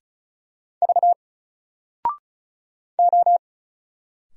Morse code, heard as a keyed tone, about 36 wpm, 700 Hz. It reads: V O
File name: Single-Letter-Number-35wpm